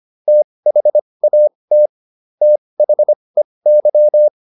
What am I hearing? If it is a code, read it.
Morse code, heard as a keyed tone, about 25 wpm, spaced as for 20 wpm, 600 Hz. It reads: THAT THEY